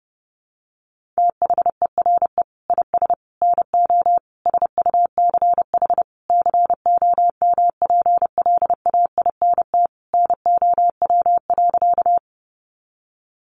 Morse code, heard as a keyed tone, about 30 wpm, 700 Hz. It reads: THERE IS NO SUCH COMPLAINT NOW.
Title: Sentences from Top 1400 Words 30wpm